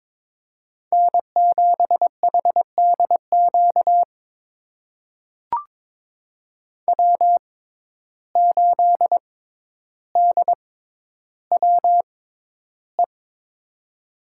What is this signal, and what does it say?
Morse code, heard as a keyed tone, about 22 wpm, spaced as for 11 wpm, 700 Hz. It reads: N7HDQ W8DWE
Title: USA Call Signs - 5 Characters - 6x Character Spacing - 22wpm